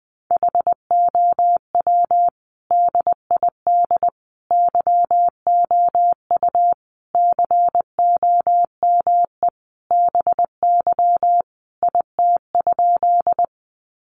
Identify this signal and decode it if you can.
Morse code, heard as a keyed tone, about 20 wpm, 700 Hz. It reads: HOW DID YOU COME BY IT?
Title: Sentences from Top 100 Words 20wpm